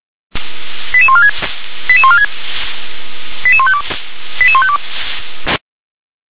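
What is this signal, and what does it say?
Five-tone selective call (ZVEI), heard as quick runs of five tones, tones of about 70 ms. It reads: 88146 88146 88152 88152
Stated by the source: ZVEI five-tone selective call 88747 received on SDR.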